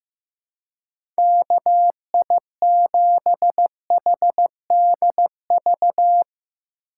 Morse code, heard as a keyed tone, about 15 wpm, 700 Hz. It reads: KI7HDV
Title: USA Call Signs - 6 Characters - 2x Character Spacing - 15wpm